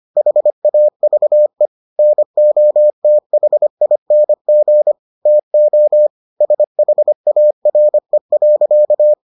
Morse code, heard as a keyed tone, about 25 wpm, 600 Hz. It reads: HAVE NOTHING TO SHARE.